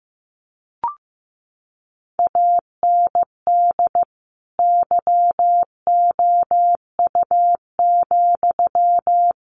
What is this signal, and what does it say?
Morse code, heard as a keyed tone, about 15 wpm, 700 Hz. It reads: AND YOU,
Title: Sentences from Top 1100 Words 15wpm